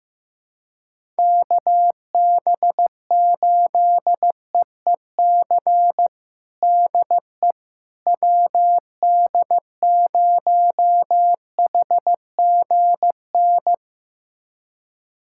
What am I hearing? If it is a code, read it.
Morse code, heard as a keyed tone, about 15 wpm, 700 Hz. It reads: KB8EEC DE WD0HGN
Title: USA Callsign DE Callsign - 6 Characters - 15wpm